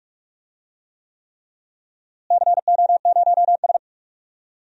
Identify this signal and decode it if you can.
Morse code, heard as a keyed tone, about 46 wpm, 700 Hz. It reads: CO0S
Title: DXCC-100 Call Signs - 4 Characters - Repeated 3x - 45wpm